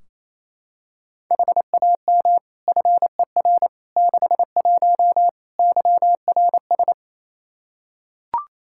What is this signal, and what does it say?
Morse code, heard as a keyed tone, about 27 wpm, 700 Hz. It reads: HAM FER 61 YRS
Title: K1USN SST QSO Practice 28wpm